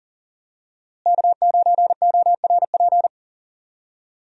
Morse code, heard as a keyed tone, about 40 wpm, 700 Hz. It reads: K9ORP
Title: USA Call Signs - 5 Characters - Repeated 2x - 40wpm